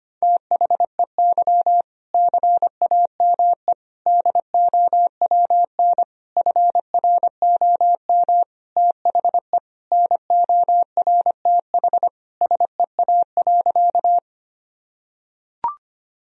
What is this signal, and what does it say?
Morse code, heard as a keyed tone, about 25 wpm, 700 Hz. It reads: THEY CAME DOWN FROM THE NORTH SEA.
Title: Sentences from Top 1000 Words - Review Part 1 - 25wpm